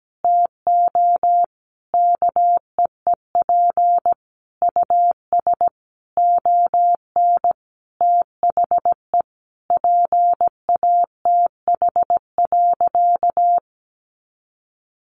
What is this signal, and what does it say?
Morse code, heard as a keyed tone, about 17 wpm, 700 Hz. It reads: TO KEEP US ON THE PATH.
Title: Sentences from Top 1400 Words 17wpm